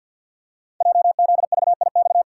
Morse code, heard as a keyed tone, about 53 wpm, 700 Hz. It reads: JZVIX
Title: Code Groups - 5 Letters - 50wpm